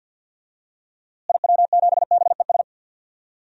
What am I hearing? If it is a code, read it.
Morse code, heard as a keyed tone, about 49 wpm, 700 Hz. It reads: IW7BES
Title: DXCC-100 Call Signs - 6 Characters - Repeated 3x - 50wpm